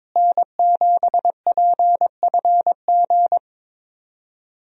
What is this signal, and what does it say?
Morse code, heard as a keyed tone, about 22 wpm, 700 Hz. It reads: N7PFG
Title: USA Call Signs - 5 Characters - Repeated 2x - 22wpm